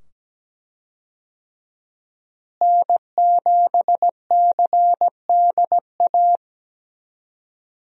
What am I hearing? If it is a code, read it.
Morse code, heard as a keyed tone, about 17 wpm, 700 Hz. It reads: N7CDA
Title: USA Call Signs - 5 Characters - Repeated 2x - 17wpm